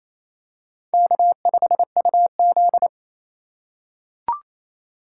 Morse code, heard as a keyed tone, about 28 wpm, 700 Hz. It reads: K5UZ
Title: USA Call Signs - 4 Characters - 3x Character Spacing - 28wpm